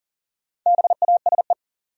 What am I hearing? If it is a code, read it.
Morse code, heard as a keyed tone, about 40 wpm, 700 Hz. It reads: BASE
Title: Sentences from Top 1000 Words - Review Part 1 - New Word Drill - 40wpm